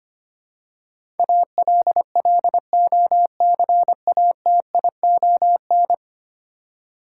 Morse code, heard as a keyed tone, about 25 wpm, 700 Hz. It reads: ALLOCATION
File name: Ten Letter Words - 25wpm